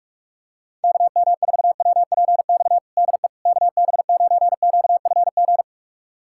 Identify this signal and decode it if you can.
Morse code, heard as a keyed tone, about 46 wpm, 700 Hz. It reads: KM4WPX DE KB9QFG
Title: USA Callsign DE Callsign - 6 Characters - 45wpm